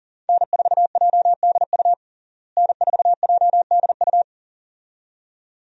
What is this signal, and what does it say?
Morse code, heard as a keyed tone, about 40 wpm, 700 Hz. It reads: N4JDU N4JDU
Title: USA Call Signs - 5 Characters - Repeated 2x - 40wpm